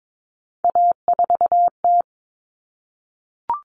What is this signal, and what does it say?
Morse code, heard as a keyed tone, about 22 wpm, 700 Hz. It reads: A4T
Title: All Cut Numbers - 3-Digit 22wpm